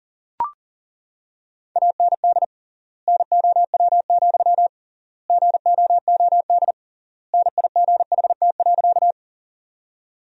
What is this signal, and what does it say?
Morse code, heard as a keyed tone, about 40 wpm, 700 Hz. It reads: AND NOW, GOOD NIGHT.
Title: 3x Word-Spacing - Sentences from Top 300 Words - 40wpm